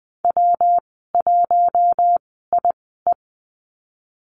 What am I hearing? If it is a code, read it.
Morse code, heard as a keyed tone, about 20 wpm, spaced as for 15 wpm, 700 Hz. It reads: W1IE
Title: 140 Lesson 23g - Letter f - Call Signs 20wpm